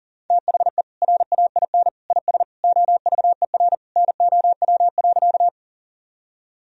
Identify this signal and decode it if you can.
Morse code, heard as a keyed tone, about 40 wpm, 700 Hz. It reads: THE RAIN IS OVER NOW.
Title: Sentences from Top 1000 Words - Review Part 1 - 40wpm